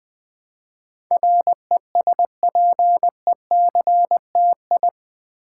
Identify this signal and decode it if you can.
Morse code, heard as a keyed tone, about 20 wpm, 700 Hz. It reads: RESPECTI
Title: Build A Word - 4 Letters At A Time - 12-Letter Words - 20wpm